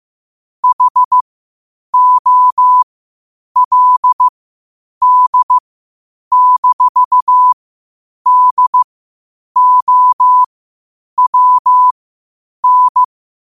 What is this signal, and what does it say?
Morse code, heard as a keyed tone, about 15 wpm, spaced as for 8 wpm, 1000 Hz. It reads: HOLD-DOWN
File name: qotdfiles / 8wpm